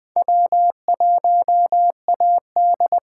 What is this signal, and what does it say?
Morse code, heard as a keyed tone, about 20 wpm, 700 Hz. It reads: W1AD